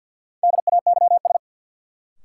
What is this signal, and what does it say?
Morse code, heard as a keyed tone, about 52 wpm, 700 Hz. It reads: DAYS
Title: Sentences from Top 1000 Words - Review Part 5 - New Word Drill - 50wpm